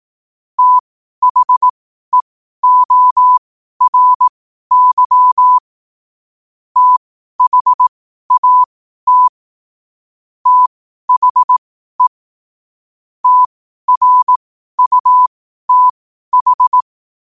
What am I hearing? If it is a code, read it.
Morse code, heard as a keyed tone, about 18 wpm, spaced as for 12 wpm, 1000 Hz. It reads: THEORY THAT THE TRUTH
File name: qotdfiles / 12wpm